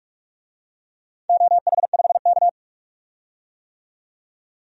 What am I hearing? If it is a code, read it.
Morse code, heard as a keyed tone, about 46 wpm, 700 Hz. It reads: OH5K
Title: DXCC-100 Call Signs - 4 Characters - 45wpm